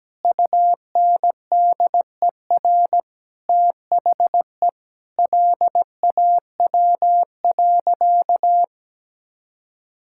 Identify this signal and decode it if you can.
Morse code, heard as a keyed tone, about 17 wpm, 700 Hz. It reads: UNDER THE LAW.
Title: Sentences from Top 1000 Words - Review Part 5 - 17wpm